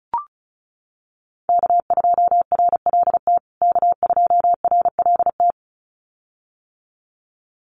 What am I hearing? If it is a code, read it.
Morse code, heard as a keyed tone, about 34 wpm, 700 Hz. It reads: K2RLT K2RLT
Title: USA Call Signs Repeated 2x 35wpm